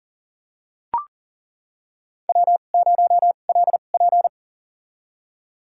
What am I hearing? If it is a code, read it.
Morse code, heard as a keyed tone, about 40 wpm, spaced as for 29 wpm, 700 Hz. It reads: W0LP
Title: USA Call Signs - 4 Characters - 2x Character Spacing - 40wpm